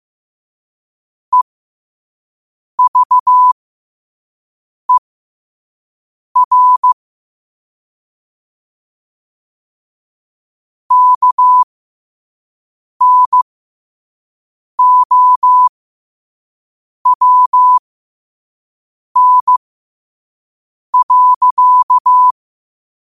Morse code, heard as a keyed tone, about 15 wpm, spaced as for 5 wpm, 1000 Hz. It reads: EVER KNOWN.